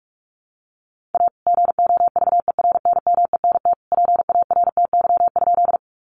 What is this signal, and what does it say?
Morse code, heard as a keyed tone, about 44 wpm, 700 Hz. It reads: A GOVERNMENT PARTY?
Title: Sentences from Top 1400 Words 45wpm